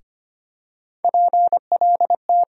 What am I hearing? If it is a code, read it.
Morse code, heard as a keyed tone, about 25 wpm, 700 Hz. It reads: PLT
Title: USA Call Sign Postfixes - 3 Letters 25wpm